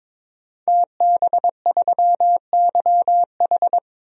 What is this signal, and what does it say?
Morse code, heard as a keyed tone, about 22 wpm, 700 Hz. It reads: TB3YH
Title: DXCC-100 Call Signs - 5 Characters - 22wpm